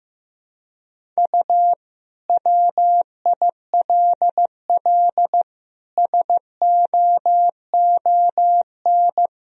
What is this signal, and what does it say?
Morse code, heard as a keyed tone, about 15 wpm, 700 Hz. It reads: U WILL SOON
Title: Sentences from Top 400 Words and Abbreviations 15wpm